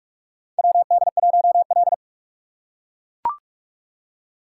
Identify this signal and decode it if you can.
Morse code, heard as a keyed tone, about 42 wpm, 700 Hz. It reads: WD1L
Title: USA Call Signs - 4 Characters - 6x Character Spacing - 45wpm